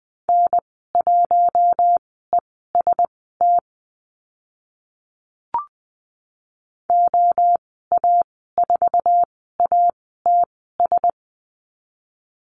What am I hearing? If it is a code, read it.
Morse code, heard as a keyed tone, about 20 wpm, spaced as for 14 wpm, 700 Hz. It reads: N1EST OA4ATS
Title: 082 Lesson 14g - Number 2 - Call Signs 20wpm